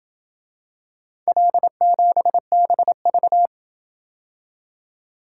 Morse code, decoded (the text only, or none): L7BV